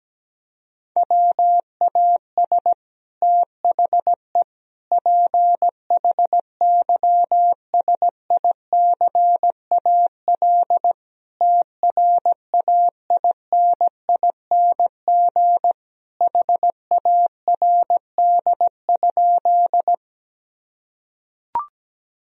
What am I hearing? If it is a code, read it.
Morse code, heard as a keyed tone, about 17 wpm, 700 Hz. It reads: WAS THE PHYSICAL TRAINING HARD?
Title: Sentences from Top 1000 Words - Review Part 4 - 17wpm